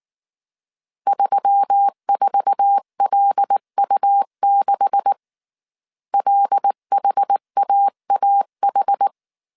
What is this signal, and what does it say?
Morse code, heard as a keyed tone, about 19 wpm, 800 Hz. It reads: 34LU6 LHAAH
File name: CW / DET / 1 9 1500C S B 18wpm 2551